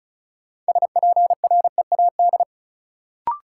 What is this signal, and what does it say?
Morse code, heard as a keyed tone, about 36 wpm, 700 Hz. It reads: SPREAD
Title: Sentences from Top 1200 Words - New Word Drill - 35wpm